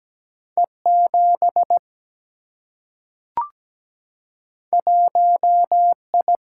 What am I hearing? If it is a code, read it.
Morse code, heard as a keyed tone, about 17 wpm, 700 Hz. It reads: E 7 1 I
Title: Single-Letter Single-Number 17wpm